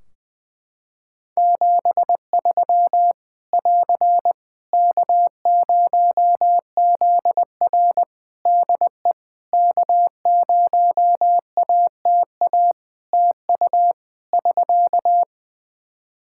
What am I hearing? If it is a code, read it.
Morse code, heard as a keyed tone, about 21 wpm, 700 Hz. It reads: 73 <AR> K0ZR DE K0ATA TU <SK>
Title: Practice QSO - Exchange 4 - Ending - 20wpm